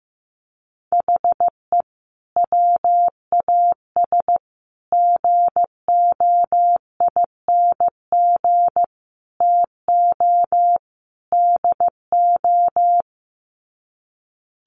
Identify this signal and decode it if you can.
Morse code, heard as a keyed tone, about 15 wpm, 700 Hz. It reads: HE WAS GOING TO DO
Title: Sets of 5 Words Encore2 15wpm